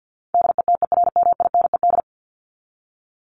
Morse code, heard as a keyed tone, about 50 wpm, 700 Hz. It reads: DETERMINED
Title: ED Suffix - Top 100 Words with Base Words 50wpm